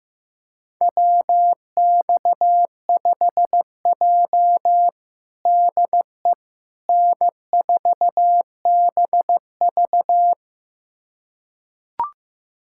Morse code, heard as a keyed tone, about 15 wpm, 700 Hz. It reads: WX5J DE N4BV
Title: USA Callsign DE Callsign - 4 Characters - 15wpm